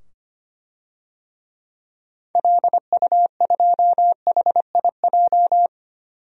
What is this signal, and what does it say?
Morse code, heard as a keyed tone, about 25 wpm, 700 Hz. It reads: LU2HIJ
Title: DXCC-100 Call Signs - 6 Characters - 25wpm